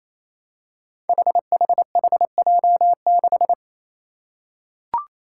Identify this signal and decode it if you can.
Morse code, heard as a keyed tone, about 28 wpm, 700 Hz. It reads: HHHJ6
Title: Five Challenging Letter-Number Combinations 28wpm